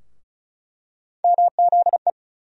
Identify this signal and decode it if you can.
Morse code, heard as a keyed tone, about 34 wpm, 700 Hz. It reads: MZE
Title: USA Call Sign Postfixes - 3 Letters 35wpm